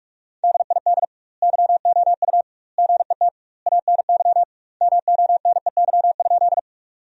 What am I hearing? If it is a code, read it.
Morse code, heard as a keyed tone, about 45 wpm, 700 Hz. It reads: DID YOU GET ANY MONEY?
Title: 2x Word-Spacing - Sentences from Top 400 Words - 45wpm